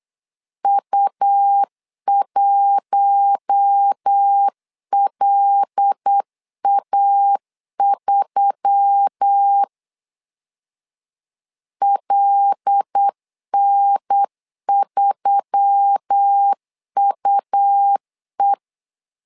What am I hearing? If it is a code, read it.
Morse code, heard as a keyed tone, about 8 wpm, 800 Hz. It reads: U1LA3 LN3UE
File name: LEZIONI / 5 LBVC / 5 L 8wpm 2831